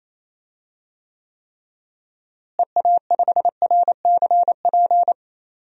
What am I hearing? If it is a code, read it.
Morse code, heard as a keyed tone, about 28 wpm, 700 Hz. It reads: EA5RCP
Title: DXCC-100 Call Signs - 6 Characters - Repeated 3x - 28wpm